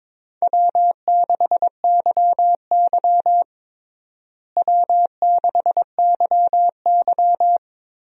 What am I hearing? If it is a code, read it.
Morse code, heard as a keyed tone, about 21 wpm, 700 Hz. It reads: W6YY W6YY
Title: USA Call Signs - 4 Characters - Repeated 2x - 22wpm